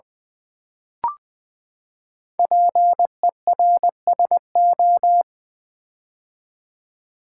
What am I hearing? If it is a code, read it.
Morse code, heard as a keyed tone, about 20 wpm, 700 Hz. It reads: PERSO
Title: Build A Word - 5 Letters At A Time - 10-Letter Words - 20wpm